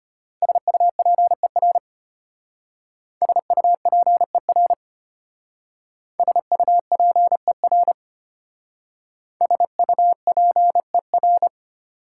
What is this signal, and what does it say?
Morse code, heard as a keyed tone, about 33 wpm, 690 Hz. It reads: SUPER SUPER SUPER SUPER